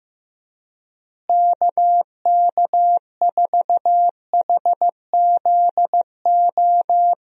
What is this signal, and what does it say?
Morse code, heard as a keyed tone, about 15 wpm, 700 Hz. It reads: KK4HZO
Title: USA Call Signs - 6 Characters - Repeated 2x - 15wpm